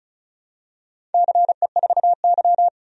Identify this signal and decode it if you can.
Morse code, heard as a keyed tone, about 34 wpm, 700 Hz. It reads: CE4Y
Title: DXCC-100 Call Signs - 4 Characters - 35wpm